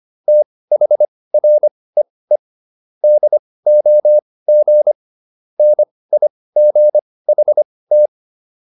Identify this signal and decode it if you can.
Morse code, heard as a keyed tone, about 25 wpm, spaced as for 18 wpm, 600 Hz. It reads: THREE DOG NIGHT